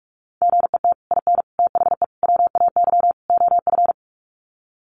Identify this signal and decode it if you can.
Morse code, heard as a keyed tone, about 44 wpm, 700 Hz. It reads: GET IN THE WAY OF
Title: Sets of 5 Words Encore 45wpm